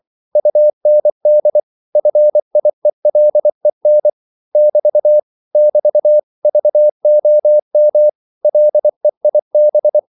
Morse code, heard as a keyed tone, about 24 wpm, 600 Hz. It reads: UND FIELEN <BT> <BT> VOM LEIB